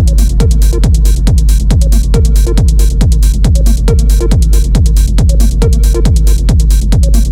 There is no code.